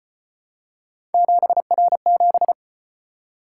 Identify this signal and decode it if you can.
Morse code, heard as a keyed tone, about 33 wpm, 700 Hz. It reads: 7R7